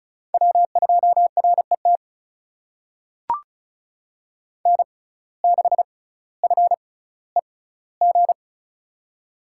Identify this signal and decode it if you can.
Morse code, heard as a keyed tone, about 36 wpm, spaced as for 18 wpm, 700 Hz. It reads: W2RET N6FEG
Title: USA Call Signs - 5 Characters - 6x Character Spacing - 35wpm